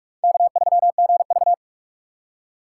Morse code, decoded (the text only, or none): K3GV